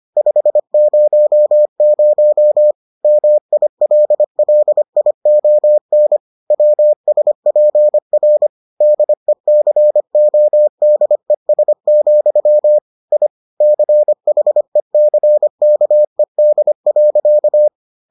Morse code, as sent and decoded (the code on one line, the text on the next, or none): ..... ----- ----- / -- .. .-.. .-.. .. --- -. / .-- ... .--. .-. / -.. -.-. --- -.. ... --..-- / .. / -.-. .... -.-. -.- . -.. .-.-.-
500 MILLION WSPR DECODES, I CHECKED.